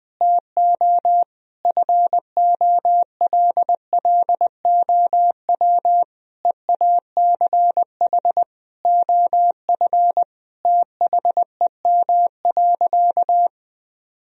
Morse code, heard as a keyed tone, about 20 wpm, 700 Hz. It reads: TO FOLLOW EACH OF THEM.
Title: Sentences from Top 1000 Words - Review Part 1 - 20wpm